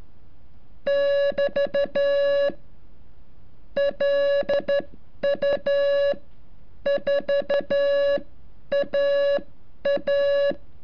Morse code, heard as a keyed tone, about 12 wpm, spaced as for 9 wpm, 560 Hz. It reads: <BT> LU4AA
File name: curso / ejercicio 33